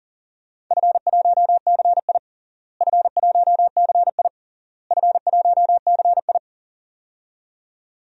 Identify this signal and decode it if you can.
Morse code, heard as a keyed tone, about 40 wpm, 700 Hz. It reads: F1CI F1CI F1CI
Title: DXCC-100 Call Signs - 4 Characters - Repeated 3x - 40wpm